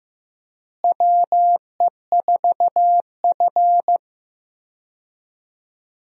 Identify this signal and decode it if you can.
Morse code, heard as a keyed tone, about 15 wpm, 700 Hz. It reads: WE4F
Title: USA Call Signs - 4 Characters - Repeated 3x - 15wpm